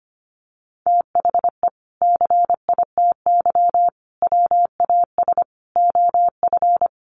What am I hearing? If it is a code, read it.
Morse code, heard as a keyed tone, about 25 wpm, 700 Hz. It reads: THE CITY WAS OF
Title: Sentences from Top 1500 Words 25wpm